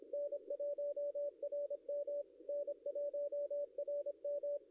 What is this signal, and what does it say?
Morse code, heard as a keyed tone, about 25 wpm, 560 Hz. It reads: N1RM N1RM
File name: contests / ss-cw-k5tr-op-n5rz -0700